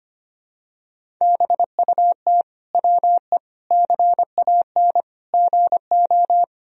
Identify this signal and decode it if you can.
Morse code, heard as a keyed tone, about 25 wpm, 700 Hz. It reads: BUT WE CAN GO
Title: Sentences from Top 100 Words and Abbreviations 25wpm